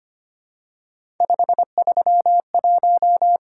Morse code, heard as a keyed tone, about 25 wpm, 700 Hz. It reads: 531
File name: Three Letter-Number Combinations - 25wpm